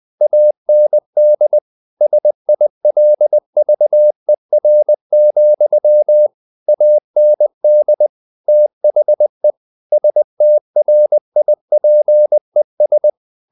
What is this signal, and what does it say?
Morse code, heard as a keyed tone, about 20 wpm, 600 Hz. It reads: AND SILVER, AND THE STRIPES
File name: the old man and the sea part 2 20wpm